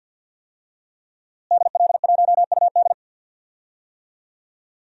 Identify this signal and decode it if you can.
Morse code, heard as a keyed tone, about 47 wpm, 700 Hz. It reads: DL1UD
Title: DXCC-100 Call Signs - 5 Characters - Repeated 3x - 50wpm